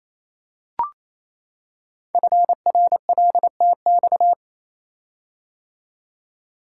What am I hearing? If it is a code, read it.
Morse code, heard as a keyed tone, about 28 wpm, 700 Hz. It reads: FRLTX
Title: Code Groups - 5 Letters - 28wpm